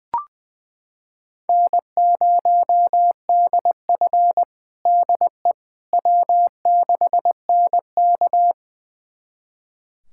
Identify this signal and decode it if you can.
Morse code, heard as a keyed tone, about 20 wpm, 700 Hz. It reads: N0DF DE W6NK
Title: USA Callsign DE Callsign - 4 Characters - 20wpm